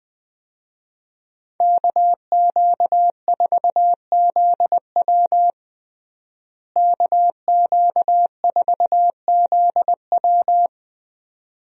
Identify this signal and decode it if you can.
Morse code, heard as a keyed tone, about 20 wpm, 700 Hz. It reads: KQ4ZW KQ4ZW
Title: USA Call Signs - 5 Characters - Repeated 2x - 20wpm